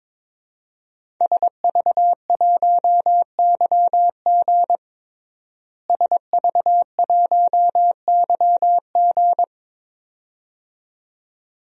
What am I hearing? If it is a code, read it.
Morse code, heard as a keyed tone, about 22 wpm, 700 Hz. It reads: SV1YG SV1YG
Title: DXCC-100 Call Signs - 5 Characters - Repeated 2x - 22wpm